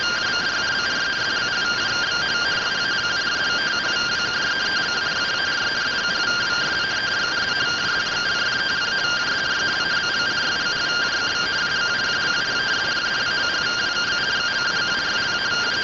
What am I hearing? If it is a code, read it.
Two-tone teleprinter signal, about 170 Hz shift.
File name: findings / 100 surview 0.gen-15 rtty rx